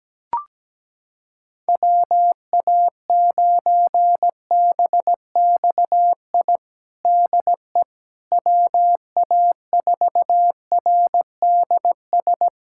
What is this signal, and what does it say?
Morse code, heard as a keyed tone, about 17 wpm, 700 Hz. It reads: WA9BXI DE WA4RDS
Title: Practice QSO - Exchange 4 - Ending - 17wpm